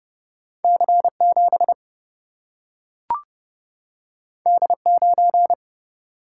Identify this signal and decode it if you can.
Morse code, heard as a keyed tone, about 30 wpm, 700 Hz. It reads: C 7 D 9
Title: Single-Letter Single-Number 30wpm